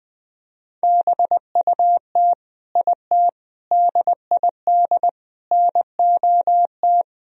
Morse code, heard as a keyed tone, about 20 wpm, 700 Hz. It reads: BUT IT DID NOT
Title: Sentences from Top 1000 Words - Review Part 2 - 20wpm